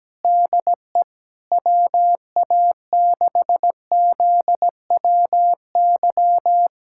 Morse code, heard as a keyed tone, about 17 wpm, 700 Hz. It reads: DE WA6ZWY
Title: USA Callsign DE Callsign - 6 Characters - 17wpm